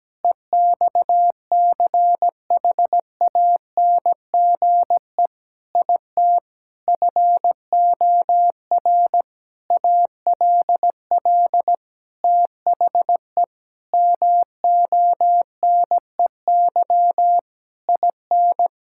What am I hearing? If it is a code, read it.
Morse code, heard as a keyed tone, about 17 wpm, 700 Hz. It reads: EXCHANGE IT FOR ALL THE MONEY IN
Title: Sentences from Top 1000 Words - Review Part 6 - 17wpm